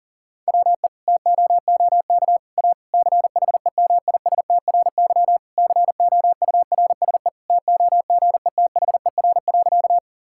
Morse code, heard as a keyed tone, about 40 wpm, 700 Hz. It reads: WE TOOK A CHEMISTRY COURSE TOGETHER.
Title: Sentences from Top 1000 Words - Review Part 6 - 40wpm